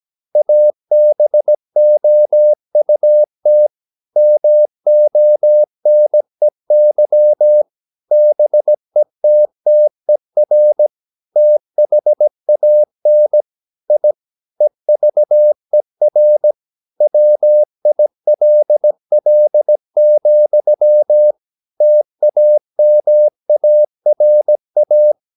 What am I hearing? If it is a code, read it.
Morse code, heard as a keyed tone, about 17 wpm, 600 Hz. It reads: ABOUT MONEY BETTER THAN I EVER WILL, TAMARA
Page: Romanen Never af Ken Follet på engelsk.